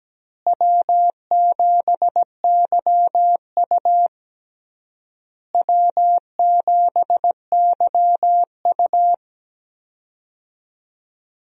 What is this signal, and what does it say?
Morse code, heard as a keyed tone, about 17 wpm, 700 Hz. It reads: W7YU W7YU
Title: USA Call Signs - 4 Characters - Repeated 3x - 17wpm